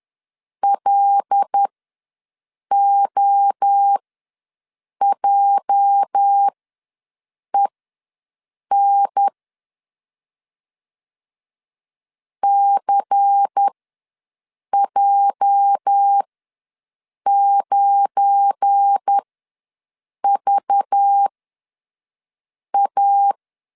Morse code, heard as a keyed tone, about 10 wpm, spaced as for 6 wpm, 800 Hz. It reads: LOJEN CJ9VA
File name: LEZIONI / 8 YJ45 / 8 YJ ws 10wpm 4043